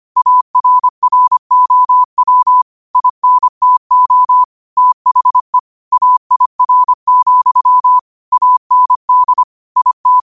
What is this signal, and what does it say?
Morse code, heard as a keyed tone, about 25 wpm, 1000 Hz. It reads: ARROW INTO THE AIR, AND IT
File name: qotdfiles / 25wpm